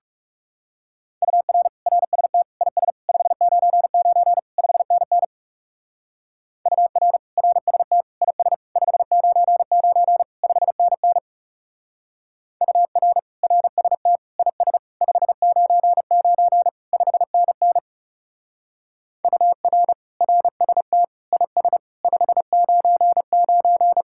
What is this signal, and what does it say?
Morse code, heard as a keyed tone, about 40 wpm, 700 Hz. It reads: UR RST IS 599 5NN UR RST IS 599 5NN UR RST IS 599 5NN UR RST IS 599